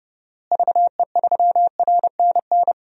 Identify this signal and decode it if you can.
Morse code, heard as a keyed tone, about 31 wpm, 700 Hz. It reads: VE3RNN
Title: DXCC-100 Call Signs - 6 Characters - Repeated 2x - 30wpm